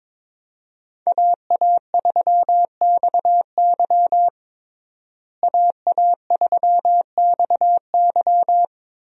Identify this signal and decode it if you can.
Morse code, heard as a keyed tone, about 22 wpm, 700 Hz. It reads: AA3XY AA3XY
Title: USA Call Signs - 5 Characters - Repeated 2x - 22wpm